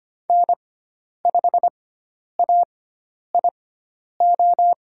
Morse code, heard as a keyed tone, about 25 wpm, spaced as for 10 wpm, 700 Hz. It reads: N5AIO
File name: Lesson 16g - Letter c - Call Signs 25wpm